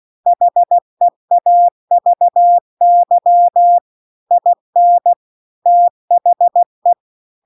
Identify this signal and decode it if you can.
Morse code, heard as a keyed tone, about 16 wpm, 700 Hz. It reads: HEAVY IN THE
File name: CW-The-Big-Twoheaded-River-2-16WPM